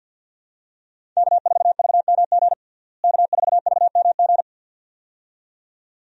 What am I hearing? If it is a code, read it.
Morse code, heard as a keyed tone, about 48 wpm, 700 Hz. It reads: K4VMG K4VMG
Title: USA Call Signs - 5 Characters - Repeated 2x - 50wpm